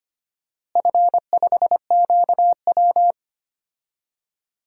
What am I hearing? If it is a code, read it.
Morse code, heard as a keyed tone, about 25 wpm, 700 Hz. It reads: F5QW
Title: DXCC-100 Call Signs - 4 Characters - Repeated 3x - 25wpm